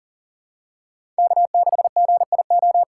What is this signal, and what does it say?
Morse code, heard as a keyed tone, about 37 wpm, 700 Hz. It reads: K6GIO
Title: USA Call Signs - 5 Characters - 2x Character Spacing - 40wpm